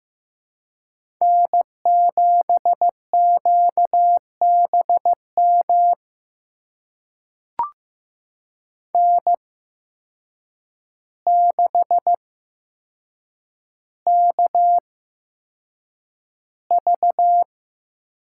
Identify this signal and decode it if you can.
Morse code, heard as a keyed tone, about 15 wpm, spaced as for 7 wpm, 700 Hz. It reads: N7QBM N6KV
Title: USA Call Signs - 5 Characters - 8x Character Spacing - 15wpm